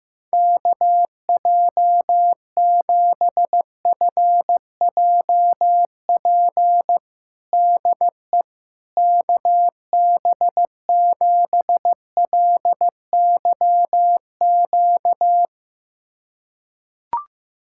Morse code, heard as a keyed tone, about 15 wpm, 700 Hz. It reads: KJ7FJP DE KB7LYQ
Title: USA Callsign DE Callsign - 6 Characters - 15wpm